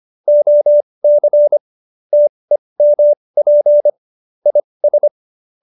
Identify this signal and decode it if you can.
Morse code, heard as a keyed tone, about 25 wpm, spaced as for 20 wpm, 590 Hz. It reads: OC TEMP IS